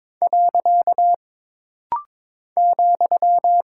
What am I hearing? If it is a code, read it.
Morse code, heard as a keyed tone, about 22 wpm, 700 Hz. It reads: . ,
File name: practice / 22wpm / Single Punctuation-Prosign - Rapid-Fire - 22wpm